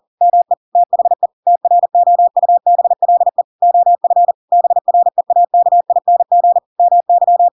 Morse code, heard as a keyed tone, about 40 wpm, 700 Hz. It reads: ME THE TROUBLE OF BREAKING MY
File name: Pinocchio / Pinocchio ch5 40 40 ik1hge